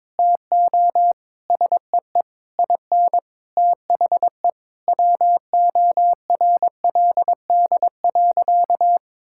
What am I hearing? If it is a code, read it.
Morse code, heard as a keyed tone, about 21 wpm, 700 Hz. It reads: TO SEE IN THE WORLD.
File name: Sentences with 9-Word Phrases - 22wpm